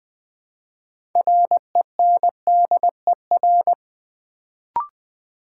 Morse code, heard as a keyed tone, about 20 wpm, 700 Hz. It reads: RENDER
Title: Build A Word - 3 Letters At A Time - 6-Letter Words - 20wpm